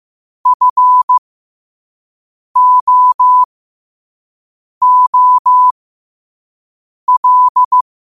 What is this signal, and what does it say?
Morse code, heard as a keyed tone, about 15 wpm, spaced as for 5 wpm, 1000 Hz. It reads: FOOL
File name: qotdfiles / 5wpm